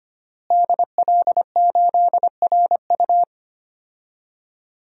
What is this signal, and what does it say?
Morse code, heard as a keyed tone, about 25 wpm, 700 Hz. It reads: DL8RU
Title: DXCC-100 Call Signs - 5 Characters - Repeated 2x - 25wpm